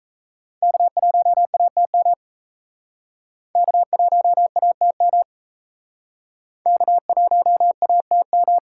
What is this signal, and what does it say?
Morse code, heard as a keyed tone, about 38 wpm, 700 Hz. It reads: K1ATM K1ATM K1ATM